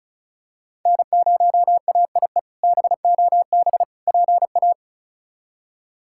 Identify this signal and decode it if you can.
Morse code, heard as a keyed tone, about 36 wpm, 700 Hz. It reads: N0AIE BOB PA
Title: K1USN SST QSO Practice - 35wpm